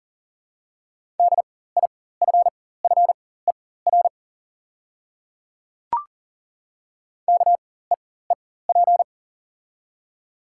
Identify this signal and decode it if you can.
Morse code, heard as a keyed tone, about 40 wpm, spaced as for 19 wpm, 700 Hz. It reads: DIFFER KEEP
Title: 4x Character Spacing - Top 500 Words - 40wpm